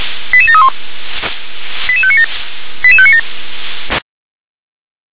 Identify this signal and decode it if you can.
Five-tone selective call (ZVEI), heard as quick runs of five tones, tones of about 70 ms. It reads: 88731 88597 88597